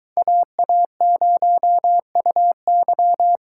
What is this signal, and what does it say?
Morse code, heard as a keyed tone, about 22 wpm, 700 Hz. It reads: AA0UY